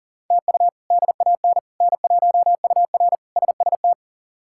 Morse code, heard as a keyed tone, about 40 wpm, 700 Hz. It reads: TU DAN N1UR SST
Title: K1USN SST QSO Practice - 40wpm